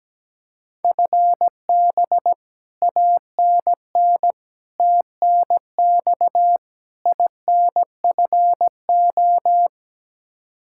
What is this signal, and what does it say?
Morse code, heard as a keyed tone, about 17 wpm, 700 Hz. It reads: FB ANN TNX INFO
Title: Practice QSO - Exchange 3 - More Info - 17wpm